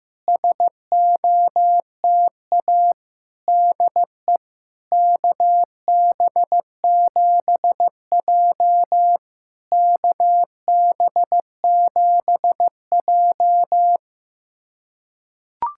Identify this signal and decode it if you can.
Morse code, heard as a keyed tone, about 15 wpm, 700 Hz. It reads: SOTA DE KB7J KB7J
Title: SOTA QSO Practice 15wpm